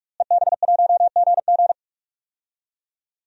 Morse code, heard as a keyed tone, about 44 wpm, 700 Hz. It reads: EB1GG
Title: DXCC-100 Call Signs - 5 Characters - Repeated 2x - 45wpm